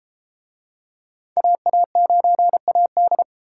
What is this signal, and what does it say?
Morse code, heard as a keyed tone, about 31 wpm, 700 Hz. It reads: AA9AD